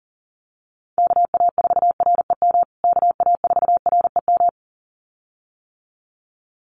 Morse code, heard as a keyed tone, about 40 wpm, 700 Hz. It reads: KA4REM KA4REM
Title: USA Call Signs Repeated 2x 40wpm